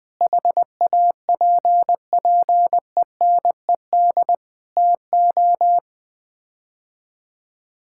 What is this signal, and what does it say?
Morse code, heard as a keyed tone, about 20 wpm, 700 Hz. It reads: HAPPENED TO
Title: Sets of 8 Words - 20wpm